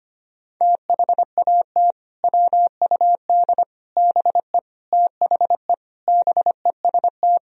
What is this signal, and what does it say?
Morse code, heard as a keyed tone, about 25 wpm, 700 Hz. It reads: THAT WUD BE THE BEST
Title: Sentences from Top 400 Words and Abbreviations 25wpm